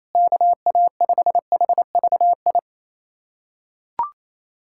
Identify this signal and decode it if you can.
Morse code, heard as a keyed tone, about 28 wpm, 700 Hz. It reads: KA5HVI
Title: USA Callsign DE Callsign - 6 Characters - 28wpm